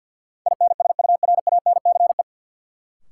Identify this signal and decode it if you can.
Morse code, heard as a keyed tone, about 52 wpm, 700 Hz. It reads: INSURANCE
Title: Sentences from Top 1100 Words - New Word Drill Alt - 50wpm